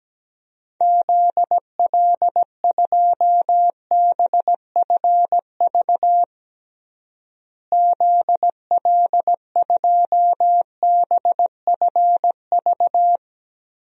Morse code, heard as a keyed tone, about 17 wpm, 700 Hz. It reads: ZL2BFV ZL2BFV
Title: DXCC-100 Call Signs - 6 Characters - Repeated 3x - 17wpm